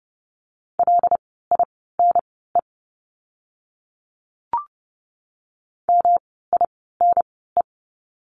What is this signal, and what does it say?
Morse code, heard as a keyed tone, about 30 wpm, spaced as for 17 wpm, 700 Hz. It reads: LINE MINE
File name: course / 30:10wpm / 108 Lesson 18f - Letter m - Words 30wpm